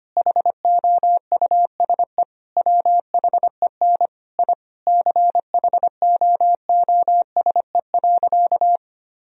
Morse code, heard as a keyed tone, about 25 wpm, 700 Hz. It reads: HOUSE WHEN I CHOOSE.
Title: Sentences from Top 1000 Words - Review Part 4 - 25wpm